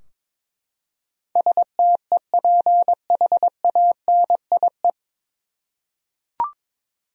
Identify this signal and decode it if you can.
Morse code, heard as a keyed tone, about 21 wpm, 700 Hz. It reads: STEPHANIE
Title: USA Common Names - 22wpm